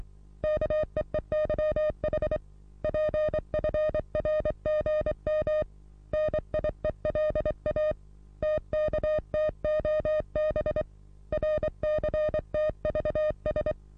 Morse code, heard as a keyed tone, about 23 wpm, 610 Hz. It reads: KEEYH PFRGM NIELA TKTOB RCTVS
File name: repos / Morsekurs / DJ4UF Morsen27